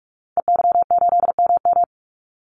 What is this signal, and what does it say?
Morse code, heard as a keyed tone, about 44 wpm, 700 Hz. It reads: EY8MM
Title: Call Signs Repeated 2x - 45wpm